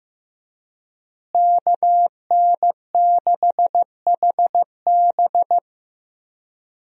Morse code, heard as a keyed tone, about 15 wpm, 700 Hz. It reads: KN6HB
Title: USA Call Signs - 5 Characters - Repeated 3x - 15wpm